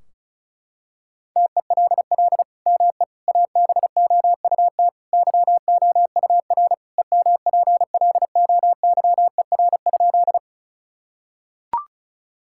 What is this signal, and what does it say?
Morse code, heard as a keyed tone, about 36 wpm, 700 Hz. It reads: TELL ME ABOUT YOUR EMPLOYER?
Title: Sentences from Top 1600 Words 35wpm